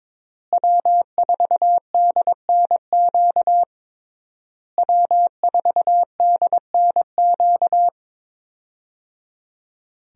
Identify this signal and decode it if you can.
Morse code, heard as a keyed tone, about 22 wpm, 700 Hz. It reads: W4DNQ W4DNQ
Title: USA Call Signs - 5 Characters - Repeated 3x - 22wpm